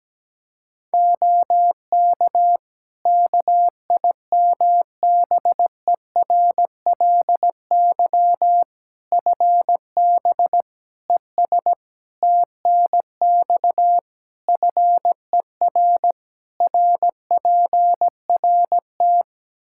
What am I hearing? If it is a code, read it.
Morse code, heard as a keyed tone, about 17 wpm, 700 Hz. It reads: OK KIMBERLY FB ES TNX FER RPRT
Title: Practice QSO - Exchange 2 - Info - 17wpm